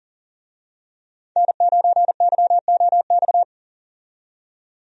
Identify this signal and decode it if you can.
Morse code, heard as a keyed tone, about 39 wpm, 700 Hz. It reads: N9YOX
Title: USA Call Signs - 5 Characters - 40wpm